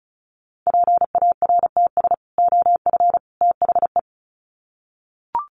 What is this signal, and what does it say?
Morse code, heard as a keyed tone, about 34 wpm, 700 Hz. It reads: PARTS OF THE
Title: Sets of 3 Words 35wpm